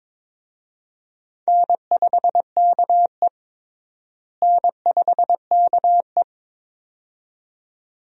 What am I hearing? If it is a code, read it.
Morse code, heard as a keyed tone, about 22 wpm, 700 Hz. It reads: N5KE N5KE
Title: USA Call Signs - 4 Characters - Repeated 2x - 22wpm